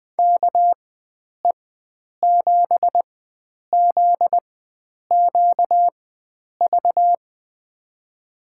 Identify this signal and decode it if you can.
Morse code, heard as a keyed tone, about 21 wpm, spaced as for 9 wpm, 700 Hz. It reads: KE7ZQV
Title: USA Call Signs - 6 Characters - 4x Character Spacing - 20wpm